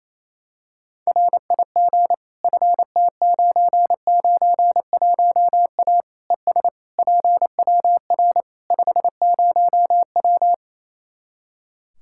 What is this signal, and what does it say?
Morse code, heard as a keyed tone, about 28 wpm, 700 Hz. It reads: RIG FT991A ES PWR 50W
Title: Practice QSO - Exchange 2 - Info - 28wpm